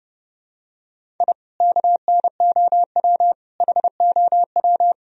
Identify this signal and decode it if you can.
Morse code, heard as a keyed tone, about 32 wpm, 700 Hz. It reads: I KNOW HOW
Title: Sentences from Top 1000 Words - Review Part 1 - 30wpm